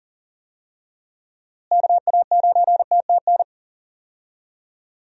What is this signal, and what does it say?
Morse code, heard as a keyed tone, about 40 wpm, 700 Hz. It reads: KA9TTN
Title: USA Call Signs - 6 Characters - Repeated 2x - 40wpm